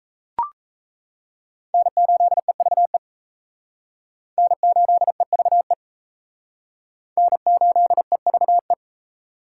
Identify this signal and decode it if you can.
Morse code, heard as a keyed tone, about 35 wpm, 700 Hz. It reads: N8EVE N8EVE N8EVE
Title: Speed Racing - USA 1x3 Call Signs 28wpm